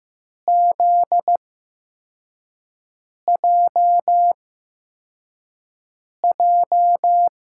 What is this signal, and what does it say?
Morse code, heard as a keyed tone, about 15 wpm, spaced as for 4 wpm, 700 Hz. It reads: ZJJ
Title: USA Call Signs - 5 Characters - 8x Character Spacing - 15wpm